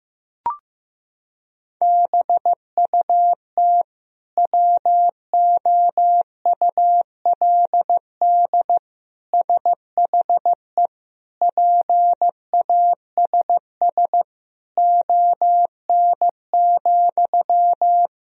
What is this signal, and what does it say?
Morse code, heard as a keyed tone, about 16 wpm, 700 Hz. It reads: BUT WOULD SHE PASS ON,
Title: Sentences from Top 1000 Words - Review Part 1 - 15wpm